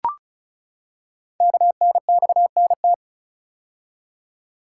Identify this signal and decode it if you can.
Morse code, heard as a keyed tone, about 36 wpm, 700 Hz. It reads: KNXNT
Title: Code Groups - 5 Letters - 35wpm